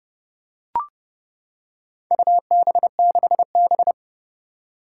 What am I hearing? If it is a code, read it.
Morse code, heard as a keyed tone, about 28 wpm, 700 Hz. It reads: UB6B
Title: Four Challenging Letter-Number Combinations 30wpm